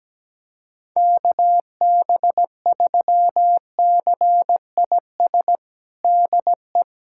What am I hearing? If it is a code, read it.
Morse code, heard as a keyed tone, about 17 wpm, 700 Hz. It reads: KB3CIS DE
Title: USA Callsign DE Callsign - 6 Characters - 17wpm